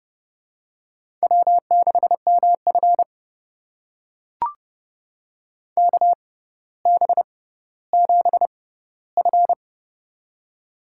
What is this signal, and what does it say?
Morse code, heard as a keyed tone, about 31 wpm, spaced as for 16 wpm, 700 Hz. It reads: W6MF KB7F
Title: USA Call Signs - 6x Character Spacing - 30wpm